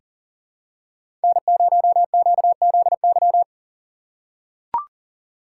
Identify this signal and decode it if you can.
Morse code, heard as a keyed tone, about 40 wpm, 700 Hz. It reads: N0QZY
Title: USA Call Signs - 5 Characters - 4x Character Spacing - 40wpm